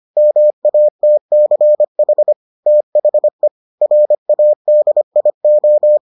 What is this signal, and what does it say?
Morse code, heard as a keyed tone, about 25 wpm, 600 Hz. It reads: MATCH THE RADIO